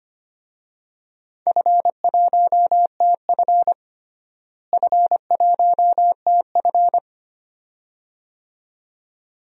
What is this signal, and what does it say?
Morse code, heard as a keyed tone, about 25 wpm, 700 Hz. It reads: F1TF F1TF
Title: DXCC-100 Call Signs - 4 Characters - Repeated 2x - 25wpm